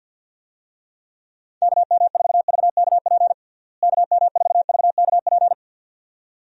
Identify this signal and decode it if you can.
Morse code, heard as a keyed tone, about 51 wpm, 700 Hz. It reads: KM4VKP KM4VKP
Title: USA Call Signs - 6 Characters - Repeated 2x - 50wpm